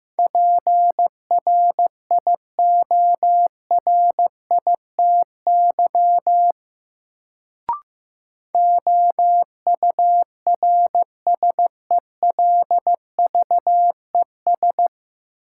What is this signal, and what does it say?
Morse code, heard as a keyed tone, about 16 wpm, 700 Hz. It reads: PRIORITY OURSELVES
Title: Sentences from Top 1600 Words - New Word Drill - 15wpm